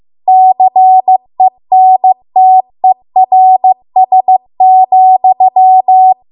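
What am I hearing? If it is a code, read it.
Morse code, heard as a keyed tone, about 15 wpm, 750 Hz. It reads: CENTERS,
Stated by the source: Morse code practice at 15 wpm from the national amateur radio association's code archive.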